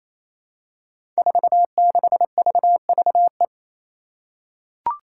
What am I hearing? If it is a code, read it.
Morse code, heard as a keyed tone, about 27 wpm, 700 Hz. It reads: 46VVE
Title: Five Challenging Letter-Number Combinations 28wpm